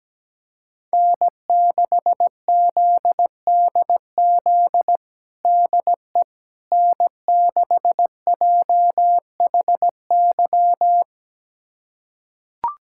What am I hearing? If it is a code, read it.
Morse code, heard as a keyed tone, about 17 wpm, 700 Hz. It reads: N6ZDZ DE N6JHY
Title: USA Callsign DE Callsign - 5 Characters - 17wpm